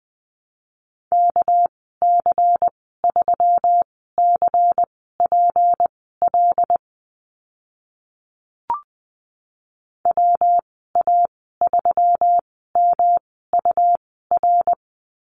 Morse code, heard as a keyed tone, about 20 wpm, spaced as for 14 wpm, 700 Hz. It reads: KC3CPL WA3MUR
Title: 207 Lesson 33g - Letter j - Call Signs 20wpm